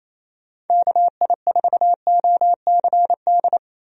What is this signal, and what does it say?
Morse code, heard as a keyed tone, about 28 wpm, 700 Hz. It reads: KI4OCD
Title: USA Call Signs - 6 Characters - 28wpm